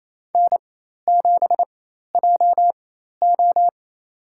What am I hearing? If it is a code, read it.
Morse code, heard as a keyed tone, about 28 wpm, spaced as for 13 wpm, 700 Hz. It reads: N7JO
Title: USA Call Signs - 4 Characters - 4x Character Spacing - 28wpm